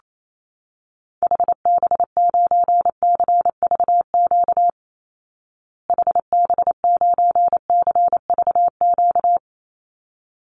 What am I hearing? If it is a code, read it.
Morse code, heard as a keyed tone, about 28 wpm, 700 Hz. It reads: HB9CVQ HB9CVQ
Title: Call Signs Repeated 3x - 28wpm